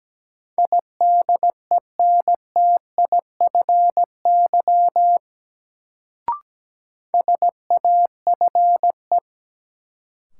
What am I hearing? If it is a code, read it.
Morse code, heard as a keyed tone, about 17 wpm, 700 Hz. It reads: IDENTIFY SAFE
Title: Sentences from Top 900 Words - New Word Drill - 17wpm